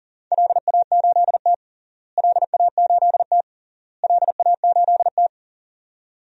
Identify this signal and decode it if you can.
Morse code, heard as a keyed tone, about 40 wpm, 700 Hz. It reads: LA8T LA8T LA8T
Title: DXCC-100 Call Signs - 4 Characters - Repeated 3x - 40wpm